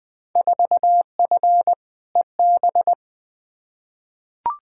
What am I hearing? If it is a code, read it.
Morse code, heard as a keyed tone, about 20 wpm, 700 Hz. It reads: 4F EB
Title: Field Day Exchanges - 20wpm